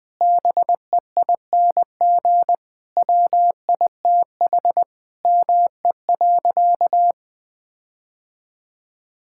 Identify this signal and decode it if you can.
Morse code, heard as a keyed tone, about 20 wpm, 700 Hz. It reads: BEING WITH ME.
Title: Sentences with 9-Word Phrases - 20wpm